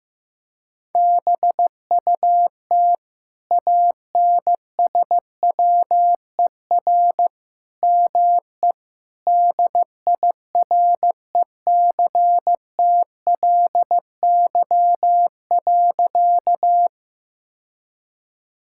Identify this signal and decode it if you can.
Morse code, heard as a keyed tone, about 15 wpm, 700 Hz. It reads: BUT ANSWER ME DIRECTLY.
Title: Sentences from Top 1000 Words - Review Part 5 - 15wpm